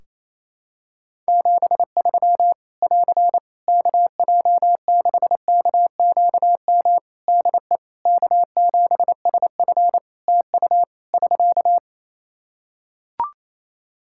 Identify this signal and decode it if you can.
Morse code, heard as a keyed tone, about 28 wpm, 700 Hz. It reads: 73 <AR> KJ6KQM DE K7SF TU <SK>
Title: Practice QSO - Exchange 4 - Ending - 28wpm